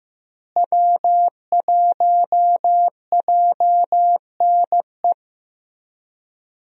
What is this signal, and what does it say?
Morse code, heard as a keyed tone, about 16 wpm, 700 Hz. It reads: W1JNE
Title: USA Call Signs - 5 Characters - Repeated 3x - 15wpm